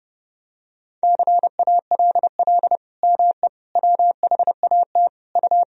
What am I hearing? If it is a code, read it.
Morse code, heard as a keyed tone, about 30 wpm, 700 Hz. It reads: CALL ME WHAT U
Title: Sentences from Top 200 Words and Abbreviations 30wpm